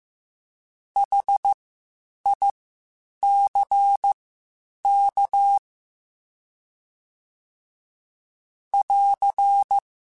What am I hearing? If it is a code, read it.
Morse code, heard as a keyed tone, about 15 wpm, spaced as for 8 wpm, 800 Hz. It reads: HICK <AR>
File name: H Prefixes / H mp3s